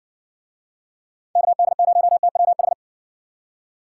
Morse code, heard as a keyed tone, about 59 wpm, 700 Hz. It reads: KD0TWH